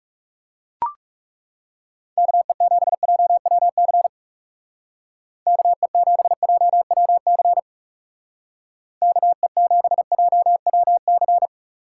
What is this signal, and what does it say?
Morse code, heard as a keyed tone, about 36 wpm, 700 Hz. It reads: KE7JWC KE7JWC KE7JWC
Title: Speed Racing - USA 2x3 Call Signs 30wpm